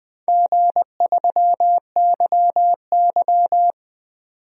Morse code, heard as a keyed tone, about 20 wpm, 700 Hz. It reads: G3YY
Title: DXCC-100 Call Signs - 4 Characters - 20wpm